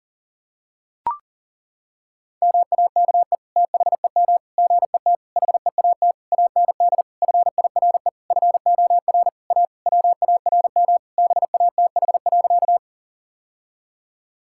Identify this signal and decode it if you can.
Morse code, heard as a keyed tone, about 37 wpm, 700 Hz. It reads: MAKE THEM GET HEAT AND FIRE FOR A WARM BATH.
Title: Sentences from Top 1600 Words 40wpm